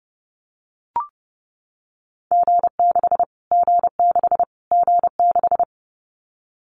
Morse code, heard as a keyed tone, about 30 wpm, 700 Hz. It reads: G6 G6 G6
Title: Callsign Prefixes for Top 25 DXCC - Repeated 3x - 30wpm